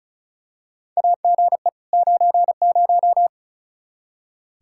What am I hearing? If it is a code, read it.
Morse code, heard as a keyed tone, about 34 wpm, 700 Hz. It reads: AGE 90
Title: Practice QSO - Exchange 3 - More Info - 35wpm